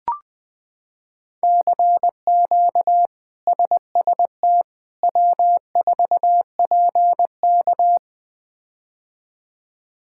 Morse code, heard as a keyed tone, about 20 wpm, 700 Hz. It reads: CQ SST W4PK
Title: K1USN SST QSO Practice - 20wpm